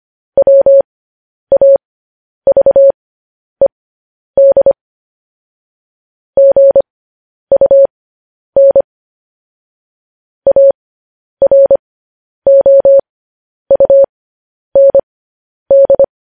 Morse code, heard as a keyed tone, about 25 wpm, spaced as for 10 wpm, 550 Hz. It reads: WAVED GUN AROUND